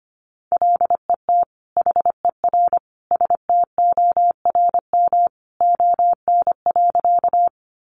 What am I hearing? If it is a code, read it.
Morse code, heard as a keyed tone, about 25 wpm, 700 Hz. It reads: LET HER STORM ON.
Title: Sentences from Top 1500 Words 25wpm